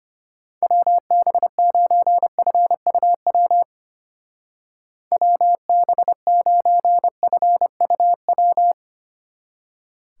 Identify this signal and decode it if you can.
Morse code, heard as a keyed tone, about 32 wpm, 700 Hz. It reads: WB9FUW WB9FUW